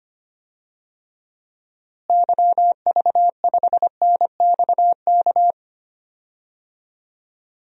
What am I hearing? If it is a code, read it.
Morse code, heard as a keyed tone, about 25 wpm, 700 Hz. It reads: YV5NXK
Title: DXCC-100 Call Signs - 6 Characters - Repeated 2x - 25wpm